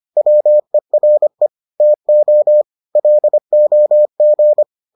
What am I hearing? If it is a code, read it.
Morse code, heard as a keyed tone, about 25 wpm, 600 Hz. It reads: WERE TO LOG